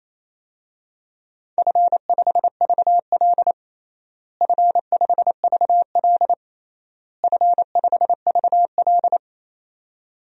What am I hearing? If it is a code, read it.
Morse code, heard as a keyed tone, about 28 wpm, 700 Hz. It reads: F5VL F5VL F5VL
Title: DXCC-100 Call Signs - 4 Characters - Repeated 3x - 28wpm